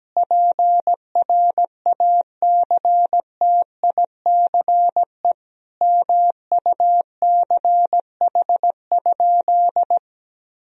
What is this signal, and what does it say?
Morse code, heard as a keyed tone, about 17 wpm, 700 Hz. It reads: PRACTICE MUCH?
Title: Sentences from Top 1000 Words - Review Part 4 - 17wpm